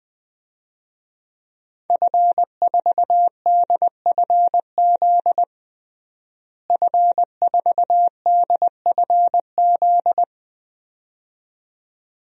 Morse code, heard as a keyed tone, about 21 wpm, 700 Hz. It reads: F4DFZ F4DFZ
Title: DXCC-100 Call Signs - 5 Characters - Repeated 2x - 20wpm